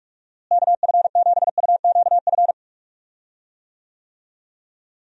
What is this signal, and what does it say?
Morse code, heard as a keyed tone, about 44 wpm, 700 Hz. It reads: KF7UQF
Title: USA Call Signs - 6 Characters - Repeated 2x - 45wpm